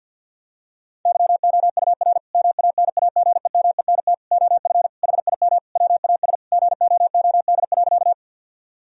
Morse code, heard as a keyed tone, about 50 wpm, 700 Hz. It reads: YOUR MANAGEMENT OF HIM WAS GOOD.
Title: Sentences from Top 1000 Words - Review Part 4 - 50wpm